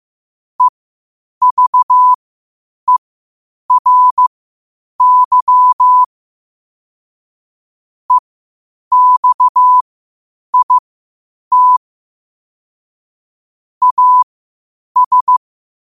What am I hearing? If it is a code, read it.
Morse code, heard as a keyed tone, about 15 wpm, spaced as for 8 wpm, 1000 Hz. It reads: EVERY EXIT AS
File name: 8wpm